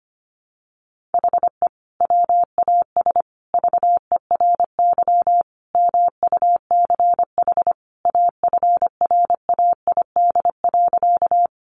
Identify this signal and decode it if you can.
Morse code, heard as a keyed tone, about 25 wpm, 700 Hz. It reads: HE WAS VERY MUCH AFRAID.
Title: Sentences from Top 1500 Words 25wpm